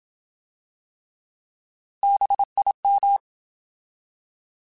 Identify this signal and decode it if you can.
Morse code, heard as a keyed tone, about 27 wpm, 800 Hz. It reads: BIM
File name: Alphabetical / Alphabetical mp3s / Bs @25